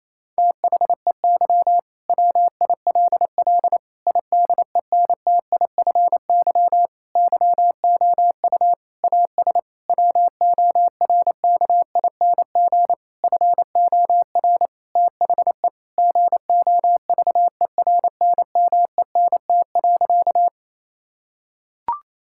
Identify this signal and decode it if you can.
Morse code, heard as a keyed tone, about 28 wpm, 700 Hz. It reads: THEY WILL IDENTIFY YOU AS WORKING FOR THE GOVERNMENT.
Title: Sentences from Top 1000 Words - Review Part 6 - 28wpm